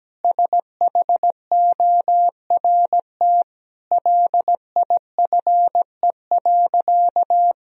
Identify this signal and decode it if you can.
Morse code, heard as a keyed tone, about 17 wpm, 700 Hz. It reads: SHORT LIFE.
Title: Sentences from Top 1000 Words - Review Part 2 - 17wpm